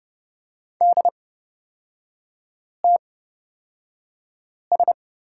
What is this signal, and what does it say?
Morse code, heard as a keyed tone, about 30 wpm, spaced as for 5 wpm, 690 Hz. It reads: DTS
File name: Lesson 37e - Letter x - Three Characters 30wpm